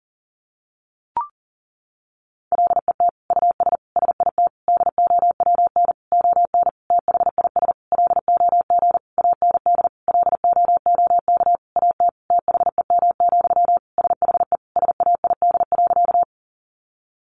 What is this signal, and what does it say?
Morse code, heard as a keyed tone, about 40 wpm, 700 Hz. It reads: LET US SIT DOWN ON THIS LOG AND LOOK AT THEM, SHE SAID.
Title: Sentences from Top 1400 Words 40wpm